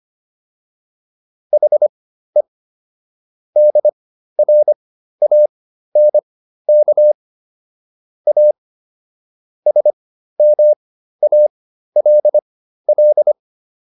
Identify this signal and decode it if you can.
Morse code, heard as a keyed tone, about 25 wpm, spaced as for 13 wpm, 600 Hz. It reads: HE DRANK A SMALL